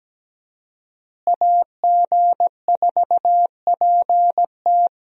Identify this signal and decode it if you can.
Morse code, heard as a keyed tone, about 17 wpm, 700 Hz. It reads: AG4PT
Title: USA Call Signs - 5 Characters - Repeated 2x - 17wpm